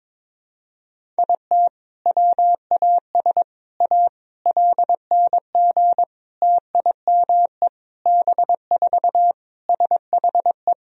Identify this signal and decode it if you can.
Morse code, heard as a keyed tone, about 22 wpm, 700 Hz. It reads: IT WAS A LNG TIME B4 SHE
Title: Sentences from Top 500 Words and Abbreviations 22wpm